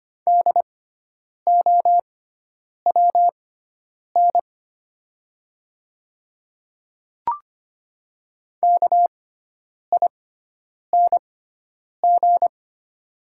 Morse code, heard as a keyed tone, about 25 wpm, spaced as for 9 wpm, 700 Hz. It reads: DOWN KING
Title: 6x Character Spacing - Top 500 Words - 25wpm